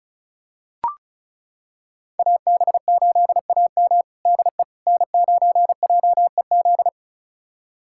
Morse code, heard as a keyed tone, about 36 wpm, 700 Hz. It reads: AB8AM DE N9JEZ
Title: USA Callsign DE Callsign - 5 Characters - 35wpm